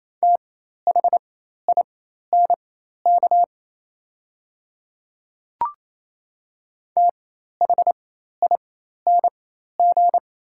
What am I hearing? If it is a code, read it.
Morse code, heard as a keyed tone, about 28 wpm, spaced as for 13 wpm, 700 Hz. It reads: THINK THING
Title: 4x Character Spacing - Top 300 Words - 28wpm